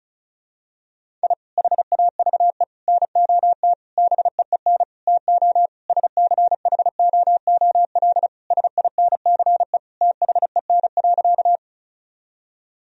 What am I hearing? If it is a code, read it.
Morse code, heard as a keyed tone, about 34 wpm, 690 Hz. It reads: I HAVE NOT BEEN TO SCHOOL SINCE THEN.
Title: Sentences from Top 1000 Words - Review Part 1 - 35wpm